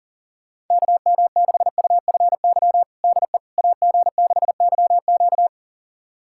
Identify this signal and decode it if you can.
Morse code, heard as a keyed tone, about 40 wpm, 700 Hz. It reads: KM6UFY DE AG6YQ
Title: USA Callsign DE Callsign 40wpm